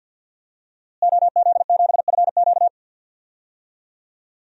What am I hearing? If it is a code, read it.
Morse code, heard as a keyed tone, about 51 wpm, 700 Hz. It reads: OZ7FQ
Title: DXCC-100 Call Signs - 5 Characters - Repeated 2x - 50wpm